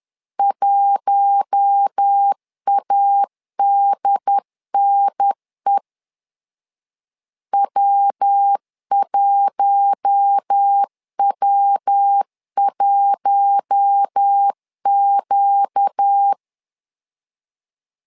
Morse code, heard as a keyed tone, about 10 wpm, 800 Hz. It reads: 1ADNE W1W1Q
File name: LEZIONI / 4 Q1 10wpm 2448